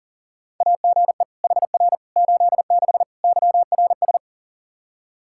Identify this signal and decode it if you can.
Morse code, heard as a keyed tone, about 40 wpm, 700 Hz. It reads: AGE HR 86 YRS
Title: Practice QSO - Exchange 3 - More Info - 40wpm